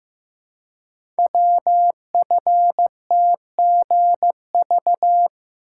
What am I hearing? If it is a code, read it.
Morse code, heard as a keyed tone, about 15 wpm, 700 Hz. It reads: WFTGV
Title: Code Groups - 5 Letters - 15wpm